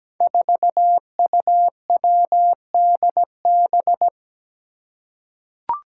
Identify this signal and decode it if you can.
Morse code, heard as a keyed tone, about 17 wpm, 700 Hz. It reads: 4UWDB
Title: Five Challenging Letter-Number Combinations 17wpm